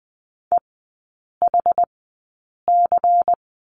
Morse code, heard as a keyed tone, about 20 wpm, spaced as for 8 wpm, 700 Hz. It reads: EHC
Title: 093 Lesson 16e - Letter c - Three Characters 20wpm